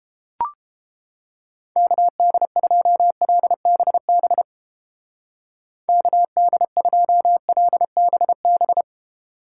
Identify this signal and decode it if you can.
Morse code, heard as a keyed tone, about 30 wpm, 700 Hz. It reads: KD2LBB KD2LBB